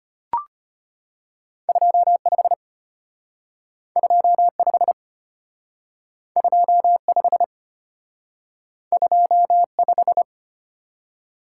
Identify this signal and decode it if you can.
Morse code, heard as a keyed tone, about 33 wpm, 700 Hz. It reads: 25 25 25 25